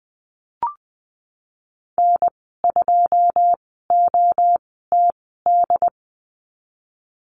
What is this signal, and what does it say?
Morse code, heard as a keyed tone, about 20 wpm, spaced as for 14 wpm, 700 Hz. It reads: N2OTD
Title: 147 Lesson 24g - Letter y - Call Signs 20wpm